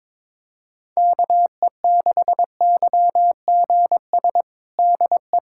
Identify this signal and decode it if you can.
Morse code, heard as a keyed tone, about 22 wpm, 700 Hz. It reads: KE6YGS DE
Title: USA Callsign DE Callsign 22wpm